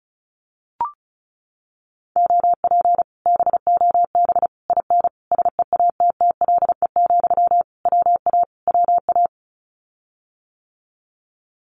Morse code, heard as a keyed tone, about 34 wpm, 700 Hz. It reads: OP BOB IN SEATTLE, WA WA
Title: Common QSO Phrases Encore 35wpm